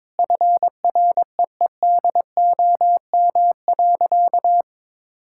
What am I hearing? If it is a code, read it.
Morse code, heard as a keyed tone, about 22 wpm, 700 Hz. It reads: FREEDOM.
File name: practice / 22wpm / Sentences with 3-Word Phrases - 22wpm